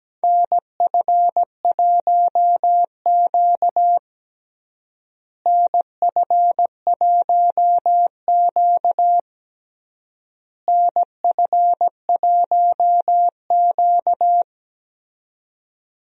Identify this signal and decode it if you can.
Morse code, heard as a keyed tone, about 17 wpm, 700 Hz. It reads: NF1Q NF1Q NF1Q
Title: USA Call Signs - 4 Characters - Repeated 3x - 17wpm